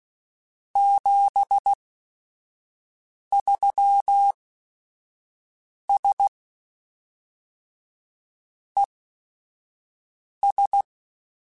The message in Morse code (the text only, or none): 73S ES